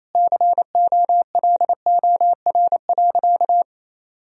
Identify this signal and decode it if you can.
Morse code, heard as a keyed tone, about 28 wpm, 700 Hz. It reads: COLOR.
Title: Sentences from Top 1000 Words - Review Part 2 - 28wpm